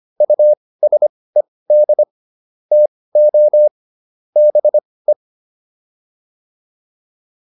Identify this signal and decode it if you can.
Morse code, heard as a keyed tone, about 25 wpm, spaced as for 18 wpm, 600 Hz. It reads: USED TO BE